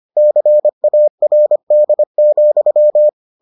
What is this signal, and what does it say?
Morse code, heard as a keyed tone, about 25 wpm, 600 Hz. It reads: CARD,